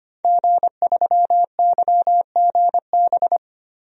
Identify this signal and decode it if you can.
Morse code, heard as a keyed tone, about 25 wpm, 700 Hz. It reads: G3YGB
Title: DXCC-100 Call Signs - 5 Characters - 25wpm